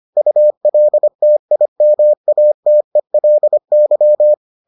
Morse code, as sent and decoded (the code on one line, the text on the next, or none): ..- .-.. - .. -- .- - . .-.. -.--
ULTIMATELY